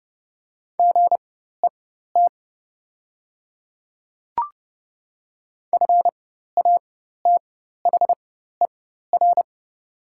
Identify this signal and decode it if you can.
Morse code, heard as a keyed tone, about 30 wpm, spaced as for 14 wpm, 700 Hz. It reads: GET FATHER
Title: 4x Character Spacing - Top 200 Words - 30wpm